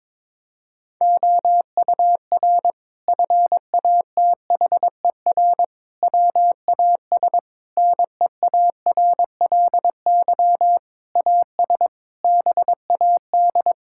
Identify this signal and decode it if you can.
Morse code, heard as a keyed tone, about 22 wpm, 700 Hz. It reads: OUR FATHER WAS NEARLY AS BAD
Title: Sentences from Top 1600 Words 22wpm